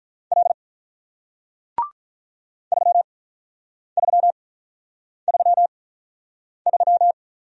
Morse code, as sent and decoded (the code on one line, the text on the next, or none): .-.. / ...-- ...-- ...-- ...--
L 3333